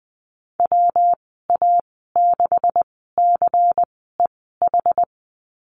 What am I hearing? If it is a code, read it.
Morse code, heard as a keyed tone, about 19 wpm, spaced as for 14 wpm, 700 Hz. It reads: WA6CEH
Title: 154 Lesson 25g - Letter p - Call Signs 20wpm